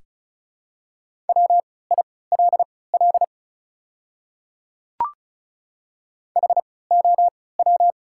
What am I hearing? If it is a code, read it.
Morse code, heard as a keyed tone, about 36 wpm, spaced as for 20 wpm, 690 Hz. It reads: WILL HOW
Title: 3x Character Spacing - Top 100 Words - 35wpm